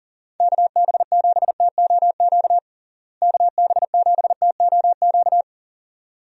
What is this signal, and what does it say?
Morse code, heard as a keyed tone, about 37 wpm, 700 Hz. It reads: KB7TOQ KB7TOQ
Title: USA Call Signs - 6 Characters - Repeated 2x - 40wpm